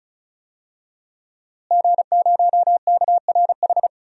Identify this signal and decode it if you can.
Morse code, heard as a keyed tone, about 34 wpm, 700 Hz. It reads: G0KRH
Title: DXCC-100 Call Signs - 5 Characters - Repeated 2x - 35wpm